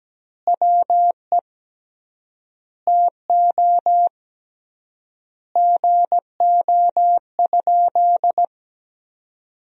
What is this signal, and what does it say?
Morse code, heard as a keyed tone, about 17 wpm, 700 Hz. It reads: WE TO GO?
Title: 3x Word-Spacing - Sentences from Top 200 Words - 17wpm